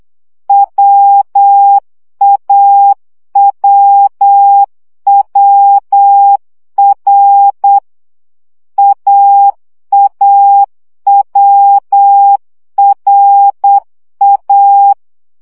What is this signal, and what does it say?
Morse code, heard as a keyed tone, about 8 wpm, 800 Hz. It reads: WAWWR AAWRA
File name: CW gyakorlas a r w 7 baud